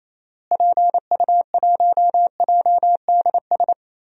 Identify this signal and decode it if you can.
Morse code, heard as a keyed tone, about 28 wpm, 700 Hz. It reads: PU1JDS
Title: DXCC-100 Call Signs - 6 Characters - Repeated 2x - 28wpm